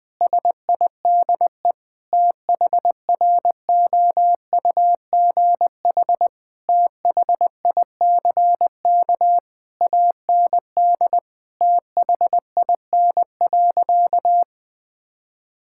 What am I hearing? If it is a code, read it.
Morse code, heard as a keyed tone, about 20 wpm, 700 Hz. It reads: SIDE THROUGH THICK AND THIN.
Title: Sentences from Top 1600 Words 20wpm